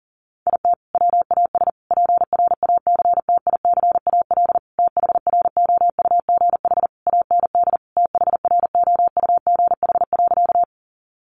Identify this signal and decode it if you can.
Morse code, heard as a keyed tone, about 40 wpm, 700 Hz. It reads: IT WAS PRACTICAL THROUGH AND THROUGH.
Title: Sentences from Top 1400 Words 40wpm